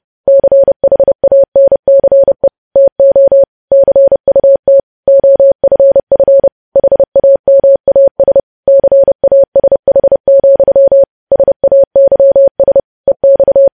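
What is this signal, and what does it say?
Morse code, heard as a keyed tone, about 30 wpm, 550 Hz. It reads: CHANCE TO CUT OFF HAMAS CASH, SAYS EX
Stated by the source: Morse code news headlines sent at 30 WPM.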